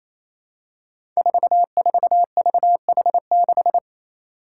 Morse code, heard as a keyed tone, about 28 wpm, 700 Hz. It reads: 44VH6
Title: Five Challenging Letter-Number Combinations 28wpm